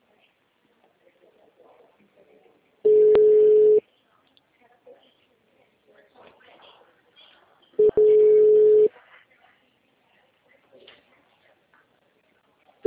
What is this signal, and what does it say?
Ringback tone (Europe, 425 Hz).